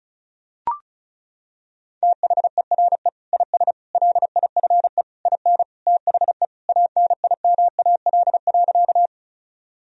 Morse code, heard as a keyed tone, about 36 wpm, 700 Hz. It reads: THERE IS LIFE IN THE ANIMAL.
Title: Sentences from Top 1000 Words - Review Part 1 - 35wpm